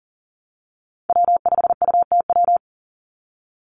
Morse code, heard as a keyed tone, about 40 wpm, 700 Hz. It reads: W5UTW
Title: Call Signs - 40wpm